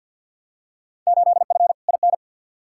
Morse code, heard as a keyed tone, about 51 wpm, 700 Hz. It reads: 8F IN
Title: Field Day Exchanges Encore - 50wpm